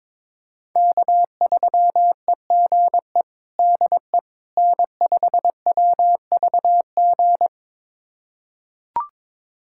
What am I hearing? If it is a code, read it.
Morse code, heard as a keyed tone, about 22 wpm, 700 Hz. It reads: K3EGE DE N5WVG
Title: USA Callsign DE Callsign - 5 Characters - 22wpm